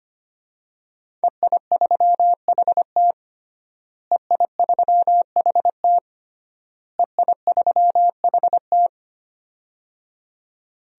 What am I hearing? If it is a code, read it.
Morse code, heard as a keyed tone, about 25 wpm, 700 Hz. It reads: EI3HT EI3HT EI3HT
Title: DXCC-100 Call Signs - 5 Characters - Repeated 3x - 25wpm